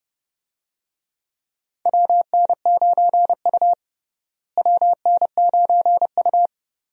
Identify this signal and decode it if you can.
Morse code, heard as a keyed tone, about 30 wpm, 700 Hz. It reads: WN9U WN9U
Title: USA Call Signs - 4 Characters - Repeated 2x - 30wpm